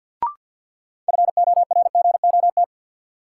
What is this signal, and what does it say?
Morse code, heard as a keyed tone, about 51 wpm, 700 Hz. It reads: FORGOT
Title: Sentences from Top 1400 Words - New Word Drill Alt - 50wpm